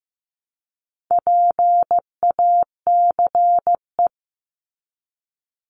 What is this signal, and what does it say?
Morse code, heard as a keyed tone, about 15 wpm, 700 Hz. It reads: PACE
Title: Top 1751-2000 Words 15wpm